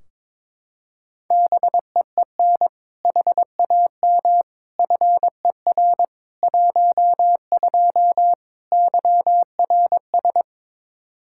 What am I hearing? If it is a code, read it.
Morse code, heard as a keyed tone, about 22 wpm, 700 Hz. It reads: BEEN HAM FER 12 YRS
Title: Practice QSO - Exchange 3 - More Info - 22wpm